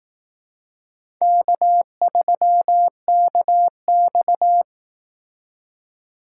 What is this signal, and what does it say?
Morse code, heard as a keyed tone, about 18 wpm, 700 Hz. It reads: K3KX
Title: Speed Racing - USA 1x2 Call Signs 15wpm